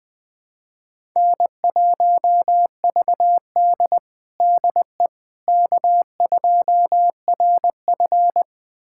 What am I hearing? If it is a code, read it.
Morse code, heard as a keyed tone, about 20 wpm, 700 Hz. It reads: N1VD DE K2RF
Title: USA Callsign DE Callsign - 4 Characters - 20wpm